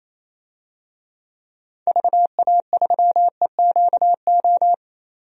Morse code, heard as a keyed tone, about 28 wpm, 700 Hz. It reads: VA3EQO
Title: DXCC-100 Call Signs - 6 Characters - 28wpm